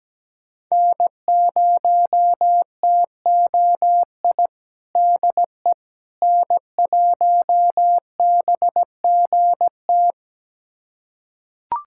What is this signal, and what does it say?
Morse code, heard as a keyed tone, about 17 wpm, 700 Hz. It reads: N0TOI DE N1BGT
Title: USA Callsign DE Callsign - 5 Characters - 17wpm